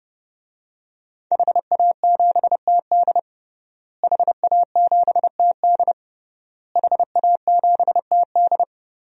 Morse code, heard as a keyed tone, about 32 wpm, 700 Hz. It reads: HA7TD HA7TD HA7TD
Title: DXCC-100 Call Signs - 5 Characters - Repeated 3x - 30wpm